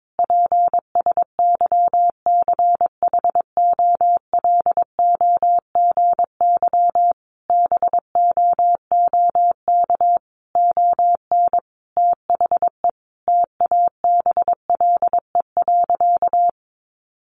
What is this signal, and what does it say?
Morse code, heard as a keyed tone, about 22 wpm, 700 Hz. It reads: PSYCHOLOGY BOOK ON THE TABLE.
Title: Sentences from Top 1400 Words 22wpm